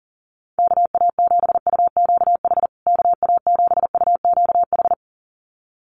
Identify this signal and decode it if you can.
Morse code, heard as a keyed tone, about 40 wpm, 700 Hz. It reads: KA7UQH KA7UQH
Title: USA Call Signs Repeated 2x 40wpm